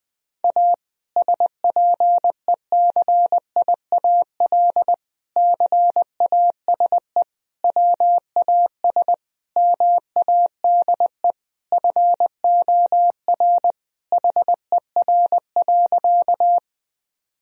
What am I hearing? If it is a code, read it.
Morse code, heard as a keyed tone, about 20 wpm, 700 Hz. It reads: A SPECIAL CASE WAS MADE FOR HER.
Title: Sentences from Top 1000 Words - Review Part 3 - 20wpm